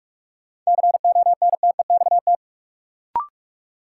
Morse code, heard as a keyed tone, about 46 wpm, 700 Hz. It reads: CONTEXT
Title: Sentences from Top 1000 Words - Review Part 7 - New Word Drill - 45wpm